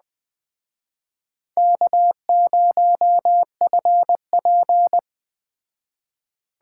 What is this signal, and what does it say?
Morse code, heard as a keyed tone, about 20 wpm, 700 Hz. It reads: K0FP